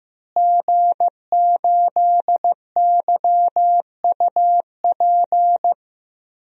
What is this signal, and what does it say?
Morse code, heard as a keyed tone, about 15 wpm, 700 Hz. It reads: G8YUP
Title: DXCC-100 Call Signs - 5 Characters - Repeated 2x - 15wpm